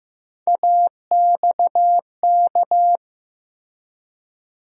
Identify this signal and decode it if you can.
Morse code, heard as a keyed tone, about 15 wpm, 700 Hz. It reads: AXK
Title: USA Call Sign Postfixes - 3 Letters 15wpm